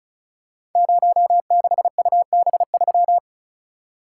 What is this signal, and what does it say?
Morse code, heard as a keyed tone, about 36 wpm, 700 Hz. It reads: 06UB3
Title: Code Groups - 5 Letter:Numbers - 35wpm